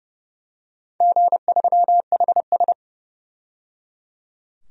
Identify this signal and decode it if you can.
Morse code, heard as a keyed tone, about 29 wpm, 700 Hz. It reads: G3HS
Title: DXCC-100 Call Signs - 4 Characters - 30wpm